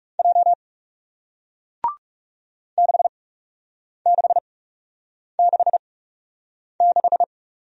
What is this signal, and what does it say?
Morse code, heard as a keyed tone, about 40 wpm, spaced as for 8 wpm, 700 Hz. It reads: J 6666